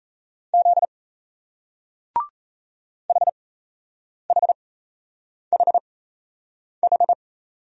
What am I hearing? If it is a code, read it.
Morse code, heard as a keyed tone, about 35 wpm, spaced as for 8 wpm, 700 Hz. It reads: Z HHHH